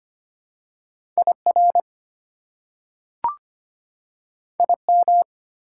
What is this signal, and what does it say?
Morse code, heard as a keyed tone, about 25 wpm, 700 Hz. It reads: IR IM
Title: Prefixes and Suffixes 25wpm